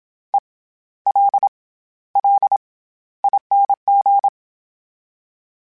Morse code, heard as a keyed tone, about 27 wpm, spaced as for 14 wpm, 800 Hz. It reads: ELLING
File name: Special INGs-mp3s / Special INGs-mp3s / SlowToFaster ing @25